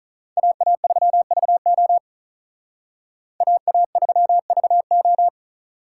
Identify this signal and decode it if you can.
Morse code, heard as a keyed tone, about 45 wpm, 700 Hz. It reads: AA3VO AA3VO